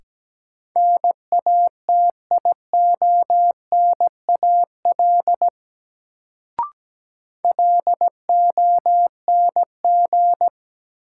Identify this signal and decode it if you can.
Morse code, heard as a keyed tone, about 17 wpm, 700 Hz. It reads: NATIONAL LONG
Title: Sets of 2 Words - New Word Drill - 17wpm